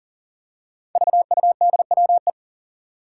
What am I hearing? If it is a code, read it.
Morse code, heard as a keyed tone, about 40 wpm, 700 Hz. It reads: VUDWE
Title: Five Challenging Letter-Number Combinations 40wpm